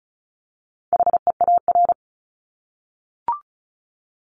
Morse code, heard as a keyed tone, about 34 wpm, 700 Hz. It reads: HEAR
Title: Top 300 Words Repeated 35wpm